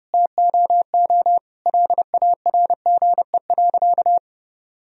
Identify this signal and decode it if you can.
Morse code, heard as a keyed tone, about 30 wpm, 700 Hz. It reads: TOO LARGE.
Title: Sentences from Top 1000 Words - Review Part 1 - 30wpm